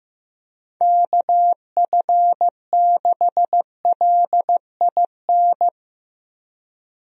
Morse code, heard as a keyed tone, about 15 wpm, 700 Hz. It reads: KF6LIN
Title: USA Call Signs - 6 Characters - Repeated 2x - 15wpm